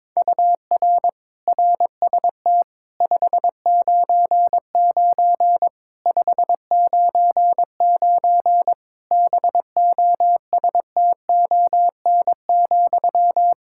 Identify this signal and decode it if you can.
Morse code, heard as a keyed tone, about 22 wpm, 700 Hz. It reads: UR RST 599 599 BOSTON,
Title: Practice QSO - Exchange 1 - Beginning - 22wpm